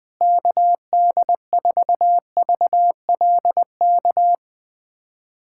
Morse code, heard as a keyed tone, about 21 wpm, 700 Hz. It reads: KD4VLK
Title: USA Callsign DE Callsign 20wpm